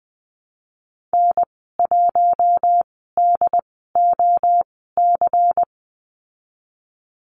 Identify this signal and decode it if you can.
Morse code, heard as a keyed tone, about 20 wpm, spaced as for 15 wpm, 700 Hz. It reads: N1DOC
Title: 213 Lesson 34g - Number 8 - Call Signs 20wpm